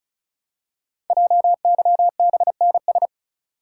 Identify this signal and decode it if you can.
Morse code, heard as a keyed tone, about 36 wpm, 700 Hz. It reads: JYBNS
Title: Code Groups - 5 Letters - 35wpm